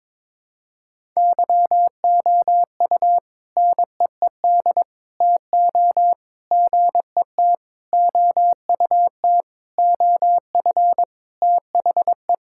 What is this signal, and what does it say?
Morse code, heard as a keyed tone, about 22 wpm, 700 Hz. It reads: YOU NEED TO GET OUT OF THE
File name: practice / 22wpm / Sentences with 4-Word Phrases - 22wpm